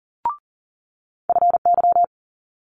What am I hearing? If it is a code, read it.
Morse code, heard as a keyed tone, about 39 wpm, 700 Hz. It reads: FY
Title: Multi-letter Phonemes Quick Fire 40wpm